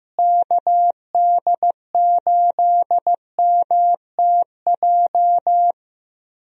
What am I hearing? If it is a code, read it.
Morse code, heard as a keyed tone, about 15 wpm, 700 Hz. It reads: KD8MTJ
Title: USA Call Signs - 6 Characters - Repeated 3x - 15wpm